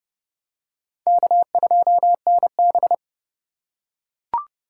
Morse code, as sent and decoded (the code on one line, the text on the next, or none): -.- ..--- -. -...
K2NB